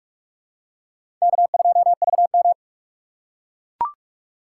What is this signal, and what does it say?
Morse code, heard as a keyed tone, about 45 wpm, 700 Hz. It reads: K2VM
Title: USA Call Signs - 4 Characters - 6x Character Spacing - 45wpm